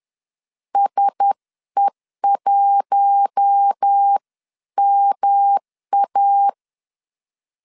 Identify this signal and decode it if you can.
Morse code, heard as a keyed tone, about 10 wpm, 800 Hz. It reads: SE1MA
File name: LEZIONI / 6 PKX0 / 6 PK 10wpm 2436